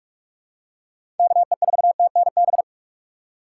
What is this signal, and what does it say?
Morse code, heard as a keyed tone, about 46 wpm, 700 Hz. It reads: KE4TNB
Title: USA Call Signs - 6 Characters - 3x Character Spacing - 45wpm